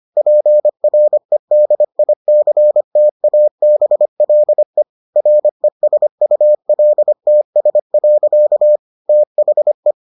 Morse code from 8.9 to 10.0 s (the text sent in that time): THE